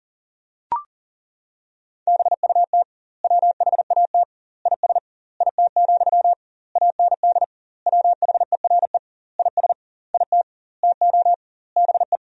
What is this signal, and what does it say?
Morse code, heard as a keyed tone, about 40 wpm, 700 Hz. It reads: BUT WHAT IS IT, AND WHERE IS IT TO BE
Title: 2x Word-Spacing - Sentences from Top 300 Words - 40wpm